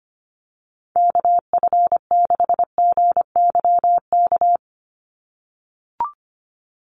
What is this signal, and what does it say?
Morse code, heard as a keyed tone, about 25 wpm, 700 Hz. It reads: KF6GYK
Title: USA Call Signs 25wpm